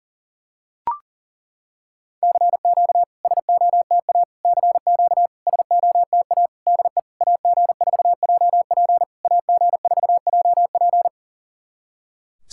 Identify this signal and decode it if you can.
Morse code, heard as a keyed tone, about 37 wpm, 690 Hz. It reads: CQ SOTA CQ SOTA DE AG4JP AG4JP
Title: SOTA QSO Practice 40wpm